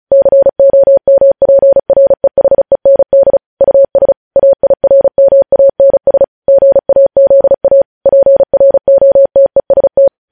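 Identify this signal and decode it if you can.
Morse code, heard as a keyed tone, about 36 wpm, 550 Hz. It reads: COMPREHEND US AIRMANS GAZA PROTEST
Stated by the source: Morse practice at 35 wpm sending news headlines.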